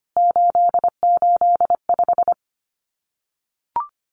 Morse code, heard as a keyed tone, about 25 wpm, 700 Hz. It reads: 885
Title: CWT Exchange 25wpm